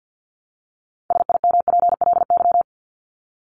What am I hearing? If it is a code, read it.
Morse code, heard as a keyed tone, about 54 wpm, 700 Hz. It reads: SIMPLY
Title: LY Suffix - Top 100 Words with Base Words 50wpm